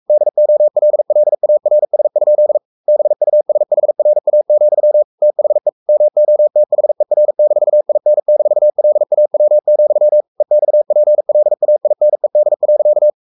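Morse code, heard as a keyed tone, about 44 wpm, 600 Hz. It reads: DOLLARS? BUSHRA, THE MOTHER-IN-LAW, EXPLAINED.
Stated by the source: Morse code at 43 wpm sending Ken Follett's novel Never in English, from CW med Gnister.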